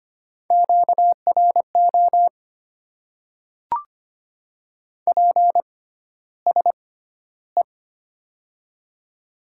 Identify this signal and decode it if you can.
Morse code, heard as a keyed tone, about 25 wpm, spaced as for 13 wpm, 700 Hz. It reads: QRO PSE
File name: Top 90 QSO Elements - 6x Character Spacing - 25wpm